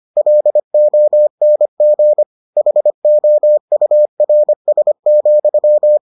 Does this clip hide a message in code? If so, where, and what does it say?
Yes: Morse code from 0.2 to 6.0 s, heard as a keyed tone, about 25 wpm, 600 Hz. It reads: LONG HOURS,